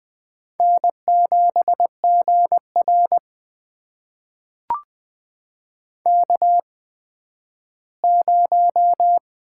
Morse code, heard as a keyed tone, about 20 wpm, spaced as for 12 wpm, 700 Hz. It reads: N7GR K0
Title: USA Call Signs - 4 Characters - 8x Character Spacing - 20wpm